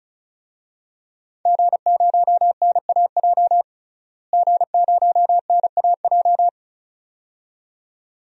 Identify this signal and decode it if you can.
Morse code, heard as a keyed tone, about 36 wpm, 700 Hz. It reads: G0NAJ G0NAJ
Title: DXCC-100 Call Signs - 5 Characters - Repeated 2x - 35wpm